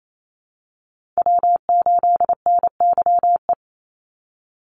Morse code, heard as a keyed tone, about 28 wpm, 700 Hz. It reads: W8NYE
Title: USA Call Signs 28wpm